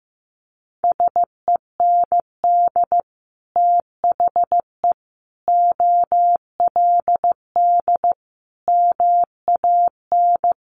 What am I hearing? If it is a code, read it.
Morse code, heard as a keyed tone, about 15 wpm, 700 Hz. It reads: SEND THE OLD MAN
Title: Sentences from Top 1500 Words 15wpm